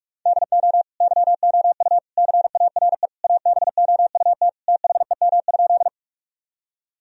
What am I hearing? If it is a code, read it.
Morse code, heard as a keyed tone, about 46 wpm, 700 Hz. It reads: DO YOU CARE ABOUT THEM?
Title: Sentences from Top 1000 Words - Review Part 1 - 45wpm